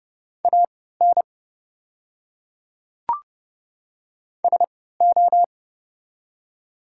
Morse code, heard as a keyed tone, about 30 wpm, spaced as for 17 wpm, 700 Hz. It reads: AN SO
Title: 3x Character Spacing - Top 200 Words - 30wpm